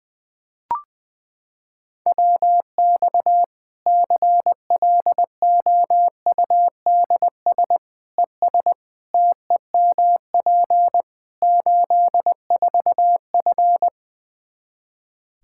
Morse code, heard as a keyed tone, about 20 wpm, 700 Hz. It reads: WX CLOUDS ES TEMP 84F
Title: Practice QSO - Exchange 2 - Info - 20wpm